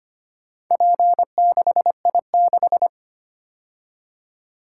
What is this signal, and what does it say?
Morse code, heard as a keyed tone, about 25 wpm, 700 Hz. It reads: P6I6
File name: practice / 25wpm / Four Letter-Number Combinations - 25wpm